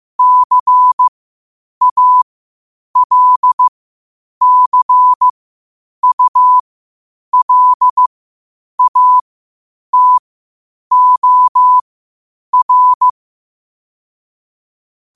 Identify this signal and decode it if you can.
Morse code, heard as a keyed tone, about 15 wpm, spaced as for 8 wpm, 1000 Hz. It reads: CALCULATOR
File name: qotdfiles / 8wpm